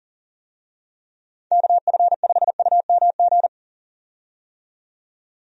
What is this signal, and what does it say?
Morse code, heard as a keyed tone, about 40 wpm, 700 Hz. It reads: KF5UMG
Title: USA Call Signs - 6 Characters - 40wpm